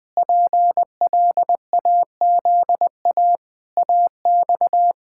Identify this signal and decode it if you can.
Morse code, heard as a keyed tone, about 21 wpm, 700 Hz. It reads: PLAZA AX
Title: Pangram Challenge v2 - 20wpm